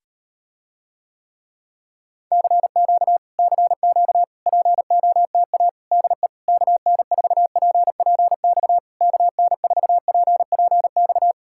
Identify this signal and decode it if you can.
Morse code, heard as a keyed tone, about 38 wpm, 700 Hz. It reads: CQ CQ POTA DE KN4PPX KN4PPX